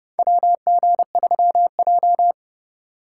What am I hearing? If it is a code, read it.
Morse code, heard as a keyed tone, about 30 wpm, 700 Hz. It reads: WG3J
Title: USA Call Signs - 4 Characters - 6x Character Spacing - 30wpm